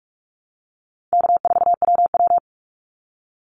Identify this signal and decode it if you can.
Morse code, heard as a keyed tone, about 46 wpm, 700 Hz. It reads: K4WW
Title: Call Signs - 45wpm